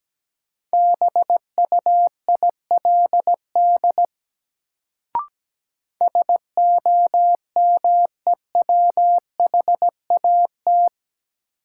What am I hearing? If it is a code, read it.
Morse code, heard as a keyed tone, about 17 wpm, 700 Hz. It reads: BUILD SOMEWHAT
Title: Sentences from Top 1000 Words - Review Part 1 - New Word Drill - 17wpm